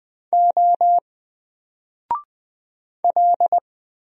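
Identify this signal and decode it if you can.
Morse code, heard as a keyed tone, about 20 wpm, 700 Hz. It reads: O L
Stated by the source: Single letter and number Morse drill at 20 wpm, from N4LCD.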